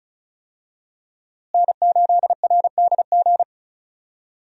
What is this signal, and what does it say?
Morse code, heard as a keyed tone, about 36 wpm, 700 Hz. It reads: N8RDG
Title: USA Call Signs - 5 Characters - Repeated 2x - 35wpm